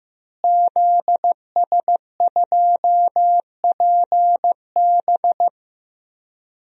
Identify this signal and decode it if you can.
Morse code, heard as a keyed tone, about 15 wpm, 700 Hz. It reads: ZS2PB
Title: DXCC-100 Call Signs - 5 Characters - Repeated 2x - 15wpm